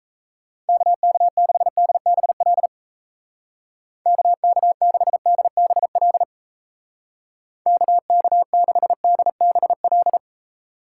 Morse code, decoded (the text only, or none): KK6DBL KK6DBL KK6DBL